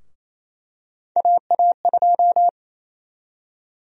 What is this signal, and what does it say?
Morse code, heard as a keyed tone, about 28 wpm, 700 Hz. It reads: AA2
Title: USA Call Sign Prefixes - Repeated 2x - 28wpm